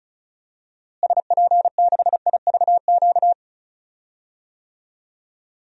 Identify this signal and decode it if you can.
Morse code, heard as a keyed tone, about 36 wpm, 700 Hz. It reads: SP6IVQ
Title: DXCC-100 Call Signs - 6 Characters - Repeated 2x - 35wpm